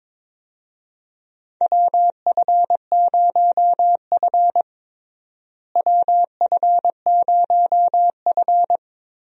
Morse code, heard as a keyed tone, about 22 wpm, 700 Hz. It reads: WF0F WF0F
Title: USA Call Signs - 4 Characters - Repeated 2x - 22wpm